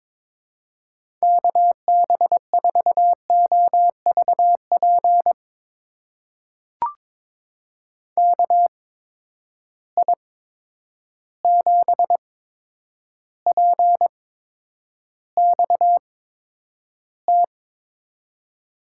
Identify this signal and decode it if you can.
Morse code, heard as a keyed tone, about 21 wpm, spaced as for 9 wpm, 700 Hz. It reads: KB4OVP KI7PXT
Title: USA Call Signs - 6 Characters - 8x Character Spacing - 22wpm